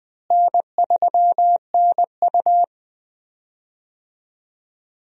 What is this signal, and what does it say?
Morse code, heard as a keyed tone, about 20 wpm, 700 Hz. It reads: N3NU
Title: USA Call Signs - 4 Characters - Repeated 3x - 20wpm